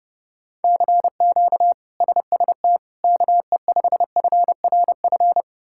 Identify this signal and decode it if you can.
Morse code, heard as a keyed tone, about 30 wpm, 700 Hz. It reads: CQ SST KE5FRF
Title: K1USN SST QSO Practice - 30wpm